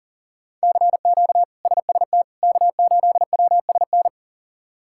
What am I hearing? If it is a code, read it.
Morse code, heard as a keyed tone, about 40 wpm, 700 Hz. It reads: CQ SST K8WSN
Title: K1USN SST QSO Practice - 40wpm